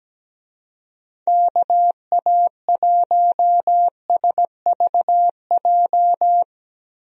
Morse code, heard as a keyed tone, about 17 wpm, 700 Hz. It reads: KA1SVJ
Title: USA Call Signs - 6 Characters - Repeated 2x - 17wpm